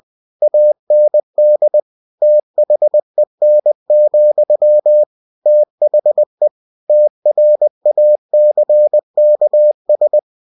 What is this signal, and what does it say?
Morse code, heard as a keyed tone, about 20 wpm, 600 Hz. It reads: AND THEN, THE TRACKS
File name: cw-books / winnie-the-pooh / winnie-20wpm-page-0024